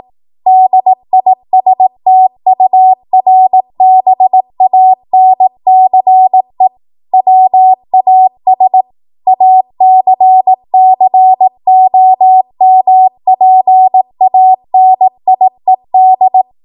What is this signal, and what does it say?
Morse code, heard as a keyed tone, about 18 wpm, 750 Hz. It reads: DISTURBANCE WAS ACCOMPANIED
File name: Archive / 18 WPM / 18WPM